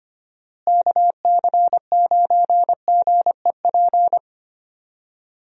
Morse code, heard as a keyed tone, about 25 wpm, 700 Hz. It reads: KC9GEP
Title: USA Call Signs - 6 Characters - Repeated 2x - 25wpm